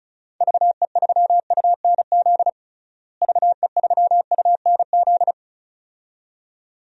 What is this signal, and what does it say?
Morse code, heard as a keyed tone, about 36 wpm, 700 Hz. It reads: VE3UNZ VE3UNZ
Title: DXCC-100 Call Signs - 6 Characters - Repeated 2x - 35wpm